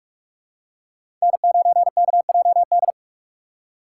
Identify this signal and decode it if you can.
Morse code, heard as a keyed tone, about 46 wpm, 700 Hz. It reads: N9KJD
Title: USA Call Signs - 4x Character Spacing - 45wpm